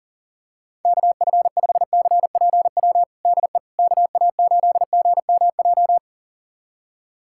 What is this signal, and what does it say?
Morse code, heard as a keyed tone, about 37 wpm, 700 Hz. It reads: KF5CPW DE KA8GMJ
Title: USA Callsign DE Callsign - 6 Characters - 40wpm